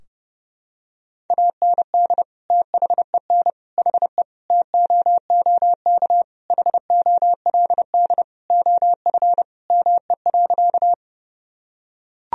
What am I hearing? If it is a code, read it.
Morse code, heard as a keyed tone, about 30 wpm, 700 Hz. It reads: AND THEN HE TOOK HOLD OF ME.
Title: Sentences from Top 1000 Words - Review Part 2 - 30wpm